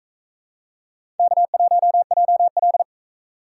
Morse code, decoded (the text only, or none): K1JL